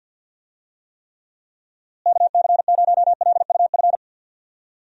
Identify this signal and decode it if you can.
Morse code, heard as a keyed tone, about 51 wpm, 700 Hz. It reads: KC0LUF
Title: USA Call Signs - 6 Characters - Repeated 2x - 50wpm